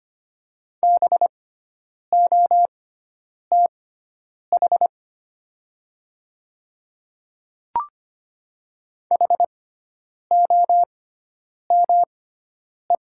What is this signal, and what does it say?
Morse code, heard as a keyed tone, about 25 wpm, spaced as for 9 wpm, 700 Hz. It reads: BOTH HOME
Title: 6x Character Spacing - Top 300 Words - 25wpm